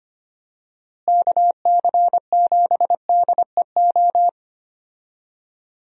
Morse code, heard as a keyed tone, about 25 wpm, 700 Hz. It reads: KC7DEO
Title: USA Call Signs - 6 Characters - Repeated 2x - 25wpm